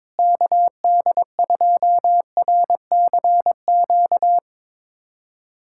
Morse code, heard as a keyed tone, about 22 wpm, 700 Hz. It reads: KD2RCQ
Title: USA Callsign DE Callsign - 6 Characters - 22wpm